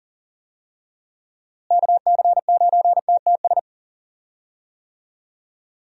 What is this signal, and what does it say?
Morse code, heard as a keyed tone, about 40 wpm, 700 Hz. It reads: KC9TTS
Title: USA Call Signs - 6 Characters - Repeated 3x - 40wpm